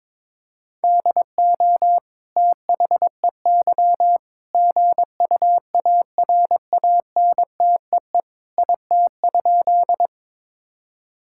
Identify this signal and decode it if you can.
Morse code, heard as a keyed tone, about 22 wpm, 700 Hz. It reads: DO THEY GUARANTEE IT?
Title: Sentences from Top 1600 Words 22wpm